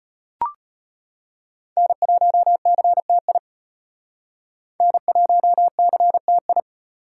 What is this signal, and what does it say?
Morse code, heard as a keyed tone, about 32 wpm, 700 Hz. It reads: N1CTI N1CTI